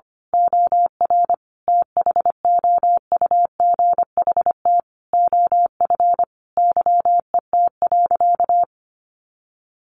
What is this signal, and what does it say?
Morse code, heard as a keyed tone, about 25 wpm, 700 Hz. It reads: OR THOUGHT OF YET.
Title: A Sound Of Thunder 25wpm